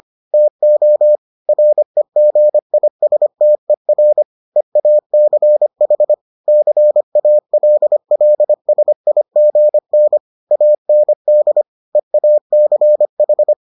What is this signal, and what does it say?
Morse code, heard as a keyed tone, about 25 wpm, 600 Hz. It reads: TO REGISTER EACH CALLSIGN AND EACH